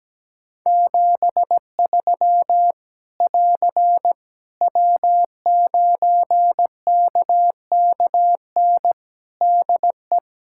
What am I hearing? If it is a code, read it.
Morse code, heard as a keyed tone, about 17 wpm, 700 Hz. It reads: 73 <AR> W9KKN DE
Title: K1USN SST QSO Practice 17wpm